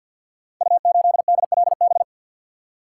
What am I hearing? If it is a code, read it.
Morse code, heard as a keyed tone, about 47 wpm, 700 Hz. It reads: U8DLB
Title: Code Groups - 5 Letter:Numbers - 50wpm